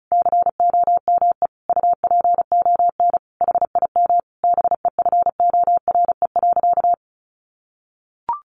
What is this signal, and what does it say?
Morse code, heard as a keyed tone, about 34 wpm, 700 Hz. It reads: COME UPON HIM BEFORE.
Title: Sentences from Top 1100 Words 35wpm